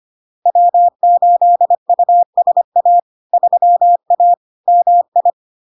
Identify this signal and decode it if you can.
Morse code, heard as a keyed tone, about 25 wpm, 700 Hz. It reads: W8USA 3A MI
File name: LARK-FD-2022-Log-in-CW-25wpm-700-Hz